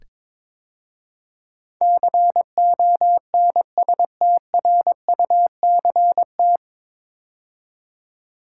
Morse code, heard as a keyed tone, about 22 wpm, 700 Hz. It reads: CONSTRUCT
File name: 9 Letter Words - First 3 Letters Given - 22wpm